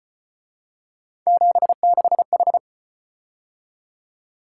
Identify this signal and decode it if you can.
Morse code, heard as a keyed tone, about 32 wpm, 700 Hz. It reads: 76H